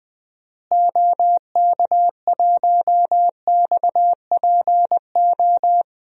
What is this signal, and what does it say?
Morse code, heard as a keyed tone, about 21 wpm, 700 Hz. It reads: OK1XPO
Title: DXCC-100 Call Signs - 6 Characters - Repeated 2x - 20wpm